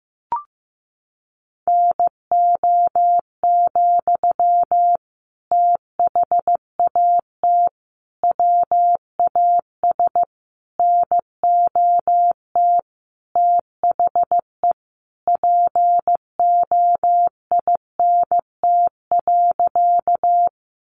Morse code, heard as a keyed tone, about 15 wpm, 700 Hz. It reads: NO, THAT WAS NOT THE POINT.
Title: Sentences from Top 600 Words 15wpm